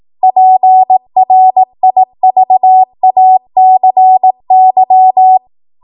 Morse code, heard as a keyed tone, about 18 wpm, 750 Hz. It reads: PRIVACY
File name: Morse / 18WPM